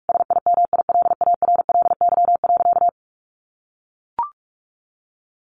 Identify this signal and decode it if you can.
Morse code, heard as a keyed tone, about 44 wpm, 700 Hz. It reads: SIMILARLY.
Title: Sentences from Top 1400 Words 45wpm